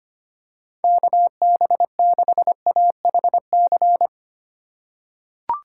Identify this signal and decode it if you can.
Morse code, heard as a keyed tone, about 25 wpm, 700 Hz. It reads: KB6AHC
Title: USA Call Signs - 6 Characters - 4x Character Spacing - 25wpm